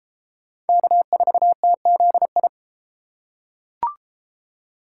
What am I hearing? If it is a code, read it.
Morse code, heard as a keyed tone, about 29 wpm, 700 Hz. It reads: K4TZI